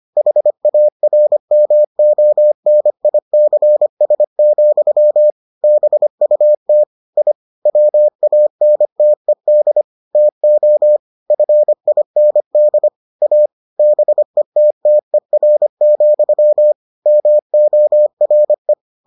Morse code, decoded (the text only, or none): HARMONICS, BUT I WANTED TO FIND A BETTER, MORE